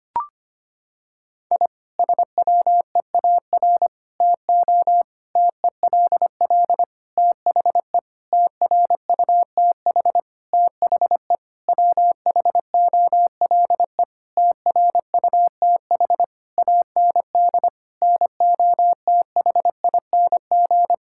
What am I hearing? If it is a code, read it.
Morse code, heard as a keyed tone, about 25 wpm, 700 Hz. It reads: I SWEAR TO TELL THE TRUTH THE WHOLE TRUTH AND NOTHING
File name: practice / 25wpm / Sentences with 9-Word Phrases - 25wpm